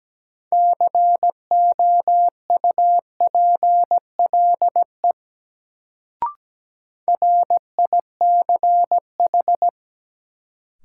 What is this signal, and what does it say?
Morse code, heard as a keyed tone, about 17 wpm, 700 Hz. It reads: COUPLE RICH
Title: Sentences from Top 1100 Words - New Word Drill - 17wpm